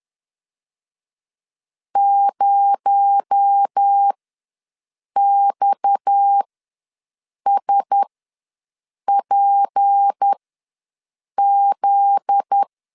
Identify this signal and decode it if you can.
Morse code, heard as a keyed tone, about 10 wpm, spaced as for 6 wpm, 800 Hz. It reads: 0XSPZ
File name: LEZIONI / 9 2 ws 10wpm 4130